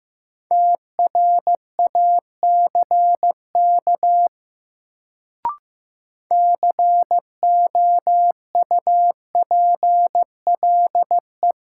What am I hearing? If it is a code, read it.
Morse code, heard as a keyed tone, about 16 wpm, 700 Hz. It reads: TRACK COUPLE
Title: Sentences from Top 1100 Words - New Word Drill - 15wpm